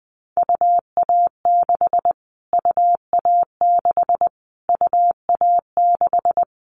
Morse code, heard as a keyed tone, about 20 wpm, 700 Hz. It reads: UA6 UA6 UA6
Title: Callsign Prefixes for Top 25 DXCC - Repeated 3x - 20wpm